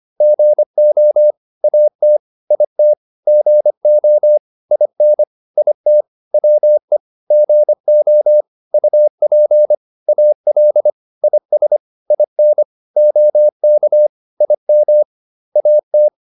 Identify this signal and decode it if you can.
Morse code, heard as a keyed tone, about 25 wpm, 600 Hz. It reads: GO AT IT GO IN IT WE GO UP AL IS IN OK IM AT